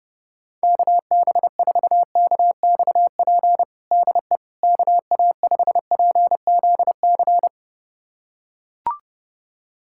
Morse code, heard as a keyed tone, about 30 wpm, 700 Hz. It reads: KB4KXP DE KA5PZC
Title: USA Callsign DE Callsign - 6 Characters - 30wpm